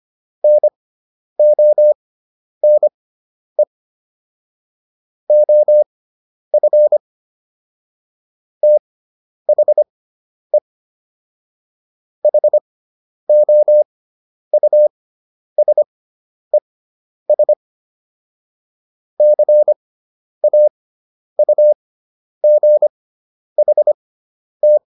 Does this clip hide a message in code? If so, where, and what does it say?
Yes: Morse code from 0.4 to 24.8 s, heard as a keyed tone, about 25 wpm, spaced as for 10 wpm, 600 Hz. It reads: NONE OF THE HOUSES CAUGHT